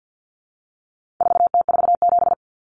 Morse code, heard as a keyed tone, about 48 wpm, 700 Hz. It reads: 4T47
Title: Common Cut Numbers - 4-Digit 50wpm